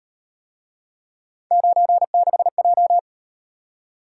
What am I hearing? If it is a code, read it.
Morse code, heard as a keyed tone, about 38 wpm, 700 Hz. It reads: 96J